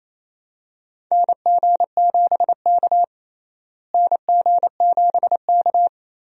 Morse code, heard as a keyed tone, about 28 wpm, 700 Hz. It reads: NG7K NG7K
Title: USA Call Signs - 4 Characters - Repeated 2x - 28wpm